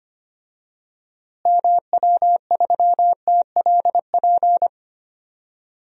Morse code, heard as a keyed tone, about 25 wpm, 700 Hz. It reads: MW3TLP
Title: DXCC-100 Call Signs - 6 Characters - Repeated 2x - 25wpm